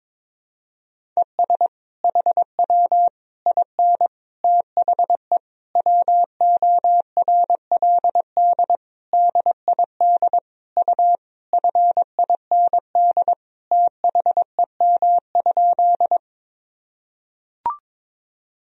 Morse code, heard as a keyed tone, about 22 wpm, 700 Hz. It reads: ES HW IN THE WORLD DID U FIND THEM?
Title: Sentences from Top 200 Words and Abbreviations 22wpm